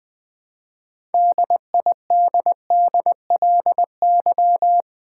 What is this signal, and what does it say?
Morse code, heard as a keyed tone, about 20 wpm, 700 Hz. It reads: DIDDLY
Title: All-Age Appropriate Jokes - 20wpm